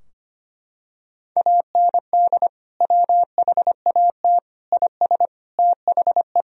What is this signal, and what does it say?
Morse code, heard as a keyed tone, about 25 wpm, 700 Hz. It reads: AND WHAT IS THE
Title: Sentences from Top 1000 Words - Review Part 3 - 25wpm